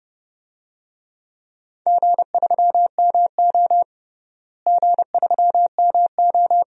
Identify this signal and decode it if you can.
Morse code, heard as a keyed tone, about 30 wpm, 700 Hz. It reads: G3MO G3MO
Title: DXCC-100 Call Signs - 4 Characters - Repeated 3x - 30wpm